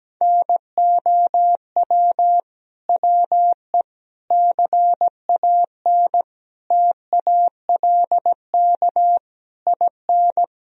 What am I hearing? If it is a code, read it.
Morse code, heard as a keyed tone, about 17 wpm, 700 Hz. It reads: NOW WE CAN TALK IN
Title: Sentences from Top 1000 Words - Review Part 4 - 17wpm